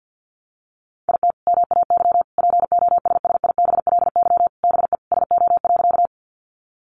Morse code, heard as a keyed tone, about 50 wpm, 700 Hz. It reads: IT MAY POSSIBLY BE SO.
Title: Sentences from Top 1300 Words 50wpm